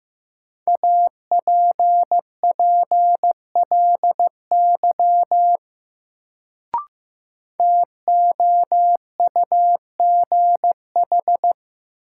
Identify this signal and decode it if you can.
Morse code, heard as a keyed tone, about 15 wpm, 700 Hz. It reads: APPLY TOUGH
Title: Sentences from Top 1200 Words - New Word Drill - 15wpm